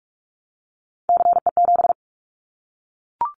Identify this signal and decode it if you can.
Morse code, heard as a keyed tone, about 44 wpm, 700 Hz. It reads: CE7
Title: Callsign Prefixes for Top 100 DXCC - 45wpm